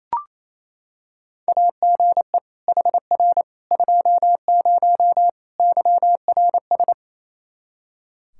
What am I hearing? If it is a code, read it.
Morse code, heard as a keyed tone, about 28 wpm, 700 Hz. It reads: AGE HR 20 YRS
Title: Practice QSO - Exchange 3 - More Info - 28wpm